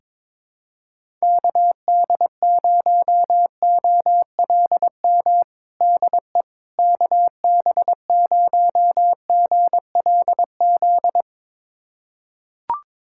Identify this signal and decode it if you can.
Morse code, heard as a keyed tone, about 22 wpm, 700 Hz. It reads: KD0OLM DE KB0GLZ
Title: USA Callsign DE Callsign - 6 Characters - 22wpm